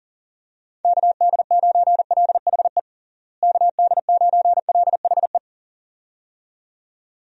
Morse code, heard as a keyed tone, about 40 wpm, 700 Hz. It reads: KD9LHE KD9LHE
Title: USA Call Signs - 6 Characters - Repeated 2x - 40wpm